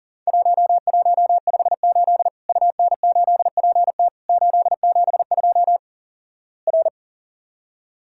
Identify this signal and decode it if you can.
Morse code, heard as a keyed tone, about 37 wpm, 700 Hz. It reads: 1158 UN8PT 872 R
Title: Worked All Europe DX Contest - QTC Exchange - 40wpm